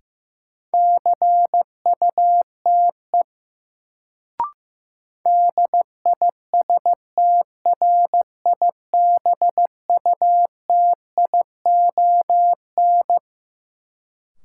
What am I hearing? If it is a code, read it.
Morse code, heard as a keyed tone, about 15 wpm, 700 Hz. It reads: CUTE DISTRIBUTION
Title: Sentences from Top 1000 Words - Review Part 7 - New Word Drill - 15wpm